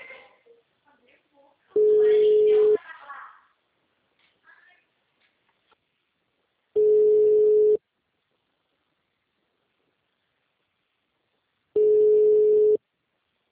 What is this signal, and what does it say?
Ringback tone (Europe, 425 Hz), with speech.